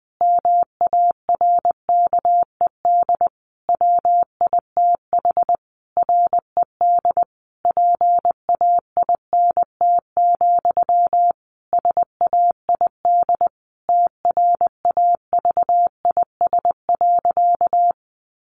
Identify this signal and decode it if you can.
Morse code, heard as a keyed tone, about 20 wpm, 700 Hz. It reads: MARKED WITH RED PAINT, SAID TRAVIS.
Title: A Sound Of Thunder 20wpm